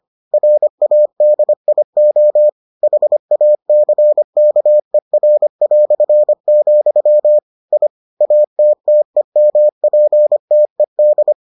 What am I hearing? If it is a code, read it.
Morse code, heard as a keyed tone, about 25 wpm, 600 Hz. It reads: RADIO HACKER", I ATTEMPTED